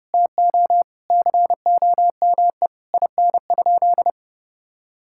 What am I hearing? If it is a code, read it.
Morse code, heard as a keyed tone, about 32 wpm, 700 Hz. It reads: TO COME IN?
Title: Sentences from Top 1600 Words 30wpm